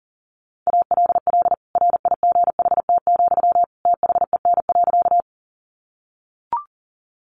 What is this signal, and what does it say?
Morse code, heard as a keyed tone, about 40 wpm, 700 Hz. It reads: ALL RIGHT, THEN.
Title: Sentences from Top 200 Words 40wpm